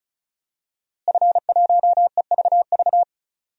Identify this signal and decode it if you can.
Morse code, heard as a keyed tone, about 34 wpm, 700 Hz. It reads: F1EVV
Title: DXCC-100 Call Signs - 5 Characters - Repeated 2x - 35wpm